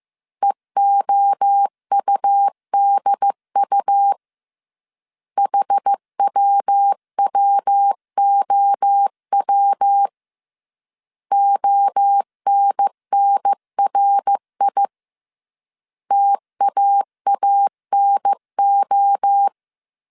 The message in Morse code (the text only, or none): EOUDU HWWOW ONNRI TAANO